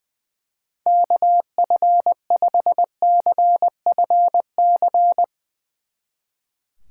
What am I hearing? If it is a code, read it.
Morse code, heard as a keyed tone, about 20 wpm, 700 Hz. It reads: KF5CFC
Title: Speed Racing - USA 2x3 Call Signs 20wpm